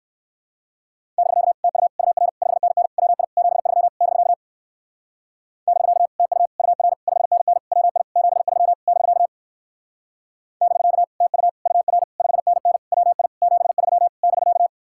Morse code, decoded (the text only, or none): <BK> TU UR 5NN WI 73 <BK> <BK> TU UR 5NN WI 73 <BK> <BK> TU UR 5NN WI 73 <BK>